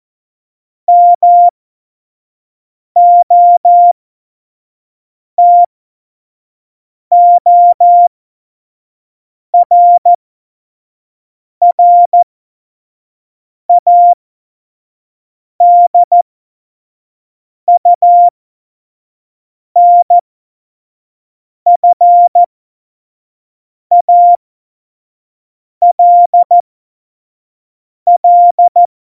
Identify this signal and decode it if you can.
Morse code, heard as a keyed tone, about 14 wpm, spaced as for 5 wpm, 700 Hz. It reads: MOTORRADUNFALL
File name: Morsekurs / Witze / Witz 29. September 2020, 25 F70